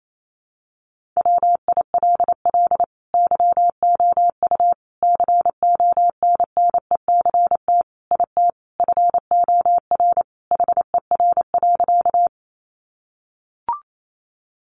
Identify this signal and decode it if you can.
Morse code, heard as a keyed tone, about 28 wpm, 700 Hz. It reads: WILL YOU CONNECT IT FOR HER.
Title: Sentences from Top 1500 Words 28wpm